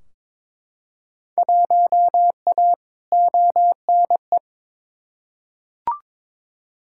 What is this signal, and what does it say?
Morse code, heard as a keyed tone, about 22 wpm, 700 Hz. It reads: 1A ONE
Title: Field Day Exchanges - 22wpm